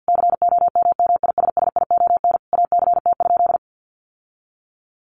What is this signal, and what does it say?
Morse code, heard as a keyed tone, about 50 wpm, 700 Hz. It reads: COMMISSION ACT?
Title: Sentences from Top 1400 Words 50wpm